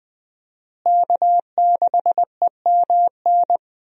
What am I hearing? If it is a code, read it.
Morse code, heard as a keyed tone, about 21 wpm, 700 Hz. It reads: K6EMN
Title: USA Call Signs - 5 Characters - Repeated 2x - 20wpm